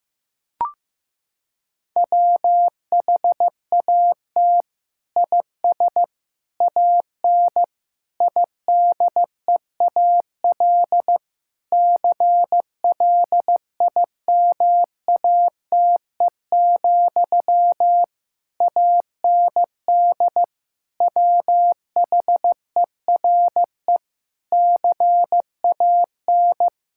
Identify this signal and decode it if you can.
Morse code, heard as a keyed tone, about 16 wpm, 700 Hz. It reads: WHAT IS AN IDEAL CLIMATE, AND WHERE CAN
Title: Sentences from Top 1600 Words 15wpm